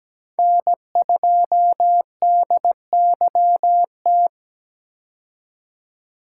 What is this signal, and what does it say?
Morse code, heard as a keyed tone, about 17 wpm, 700 Hz. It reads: N2DYT
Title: K1USN SST QSO Practice - 17wpm